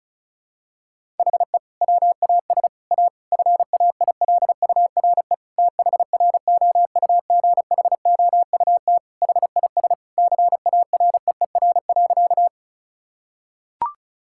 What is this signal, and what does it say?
Morse code, heard as a keyed tone, about 36 wpm, 700 Hz. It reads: HE WAS A FAILURE THROUGHOUT HIS CAREER.
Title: Sentences from Top 1000 Words - Review Part 6 - 35wpm